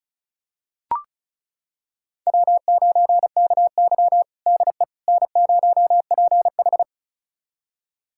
Morse code, heard as a keyed tone, about 36 wpm, 700 Hz. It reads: W9KY DE N0PH
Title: USA Callsign DE Callsign - 4 Characters - 35wpm